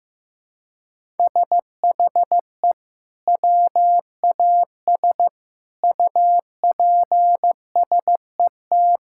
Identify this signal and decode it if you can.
Morse code, heard as a keyed tone, about 15 wpm, 700 Hz. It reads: SHE WAS UPSET
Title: Sentences from Top 1000 Words - Review Part 5 - 15wpm